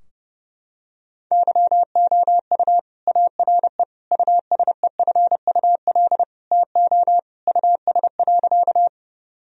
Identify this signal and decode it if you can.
Morse code, heard as a keyed tone, about 30 wpm, 700 Hz. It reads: YOU ARE USEFUL TO US.
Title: Sentences from Top 1000 Words - Review Part 4 - 30wpm